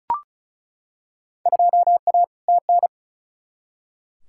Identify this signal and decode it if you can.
Morse code, heard as a keyed tone, about 36 wpm, 700 Hz. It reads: 2A TN
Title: Field Day Exchanges - 35wpm